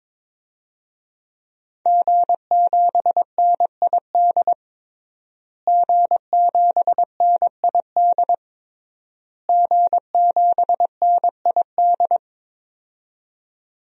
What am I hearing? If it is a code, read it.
Morse code, heard as a keyed tone, about 22 wpm, 700 Hz. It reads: G7NID G7NID G7NID
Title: DXCC-100 Call Signs - 5 Characters - Repeated 3x - 22wpm